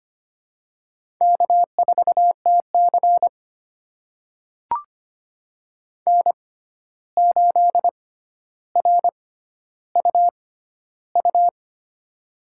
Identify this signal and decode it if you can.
Morse code, heard as a keyed tone, about 25 wpm, spaced as for 12 wpm, 700 Hz. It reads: K4TC N8RUU
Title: USA Call Signs - 6x Character Spacing - 25wpm